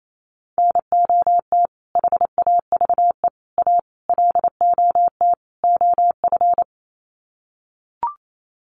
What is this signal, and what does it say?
Morse code, heard as a keyed tone, about 28 wpm, 700 Hz. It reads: NOT HAVE A LOT OF
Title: Sets of 5 Words 28wpm